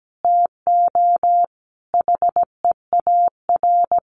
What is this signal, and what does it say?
Morse code, heard as a keyed tone, about 17 wpm, 700 Hz. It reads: TO HEAR
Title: Sets of 5 Words 17wpm